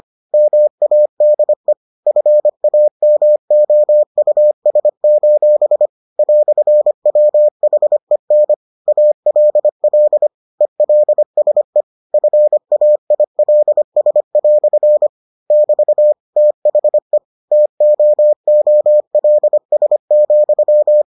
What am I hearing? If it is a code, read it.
Morse code, heard as a keyed tone, about 25 wpm, 600 Hz. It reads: MADE FAMOUS: "WHEN ALL ELSE FAILS" <BT> THE TOOLS,